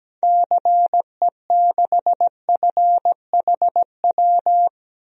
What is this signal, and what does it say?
Morse code, heard as a keyed tone, about 17 wpm, 700 Hz. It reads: CE6FHW
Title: DXCC-100 Call Signs - 6 Characters - 17wpm